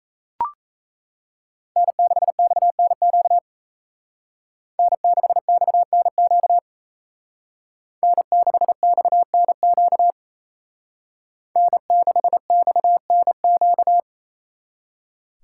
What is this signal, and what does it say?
Morse code, heard as a keyed tone, about 36 wpm, 700 Hz. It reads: N6XNQ N6XNQ N6XNQ N6XNQ